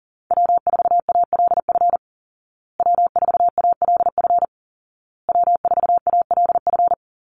Morse code, heard as a keyed tone, about 40 wpm, 700 Hz. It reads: W4ALF W4ALF W4ALF
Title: Call Signs Repeated 2x - 40wpm